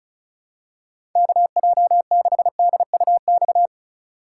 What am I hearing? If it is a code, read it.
Morse code, heard as a keyed tone, about 36 wpm, 700 Hz. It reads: KJ6DUX
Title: USA Call Signs - 6 Characters - 4x Character Spacing - 35wpm